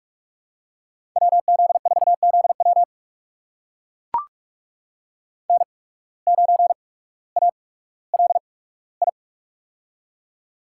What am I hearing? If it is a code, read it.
Morse code, heard as a keyed tone, about 43 wpm, spaced as for 19 wpm, 700 Hz. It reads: WZ4ZW N9ALI
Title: USA Call Signs - 5 Characters - 8x Character Spacing - 45wpm